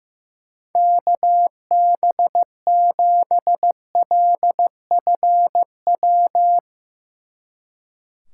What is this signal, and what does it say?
Morse code, heard as a keyed tone, about 15 wpm, 700 Hz. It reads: KB7LFW
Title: Speed Racing - USA 2x3 Call Signs 15wpm